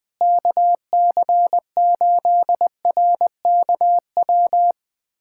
Morse code, heard as a keyed tone, about 20 wpm, 700 Hz. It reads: KC8RKW
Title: USA Callsign DE Callsign - 6 Characters - 20wpm